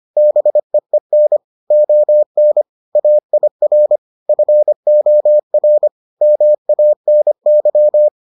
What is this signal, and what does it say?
Morse code, heard as a keyed tone, about 25 wpm, 600 Hz. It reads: BEEN ON AIR FOR MANY